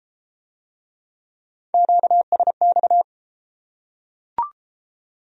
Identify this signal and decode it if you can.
Morse code, heard as a keyed tone, about 33 wpm, 700 Hz. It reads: QSX